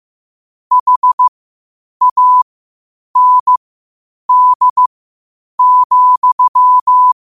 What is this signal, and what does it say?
Morse code, heard as a keyed tone, about 15 wpm, spaced as for 8 wpm, 1000 Hz. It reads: HAND,
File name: qotdfiles / 8wpm